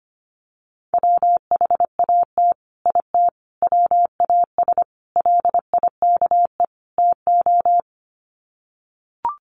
Morse code, heard as a keyed tone, about 25 wpm, 700 Hz. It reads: WHAT IT WAS LIKE TO
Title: Sets of 5 Words Encore 25wpm